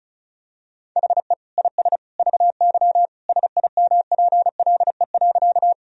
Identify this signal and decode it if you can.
Morse code, heard as a keyed tone, about 34 wpm, 700 Hz. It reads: HE IS VY SIMPLE.
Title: Sentences from Top 400 Words and Abbreviations 35wpm